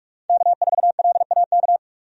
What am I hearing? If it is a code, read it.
Morse code, heard as a keyed tone, about 41 wpm, 700 Hz. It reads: K4LAK
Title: USA Call Signs - 5 Characters - 45wpm